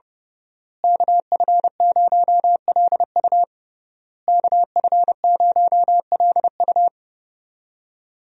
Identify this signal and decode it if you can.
Morse code, heard as a keyed tone, about 30 wpm, 700 Hz. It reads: KF0LU KF0LU
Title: USA Call Signs - 5 Characters - Repeated 3x - 30wpm